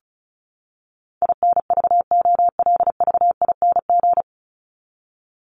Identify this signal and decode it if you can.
Morse code, heard as a keyed tone, about 34 wpm, 700 Hz. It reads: INVOLVING
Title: ING Suffix - Top 100 Words with Base Words 35wpm